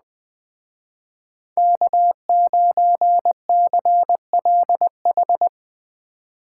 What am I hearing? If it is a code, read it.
Morse code, heard as a keyed tone, about 21 wpm, 700 Hz. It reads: K9CLH